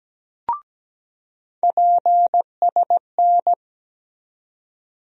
Morse code, heard as a keyed tone, about 17 wpm, 700 Hz. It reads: PSN
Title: USA Call Sign Postfixes - 3 Letters 17wpm